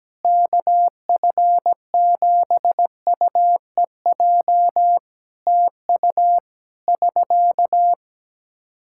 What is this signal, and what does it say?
Morse code, heard as a keyed tone, about 17 wpm, 700 Hz. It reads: KF7UEJ TU <SK>
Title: K1USN SST QSO Practice 17wpm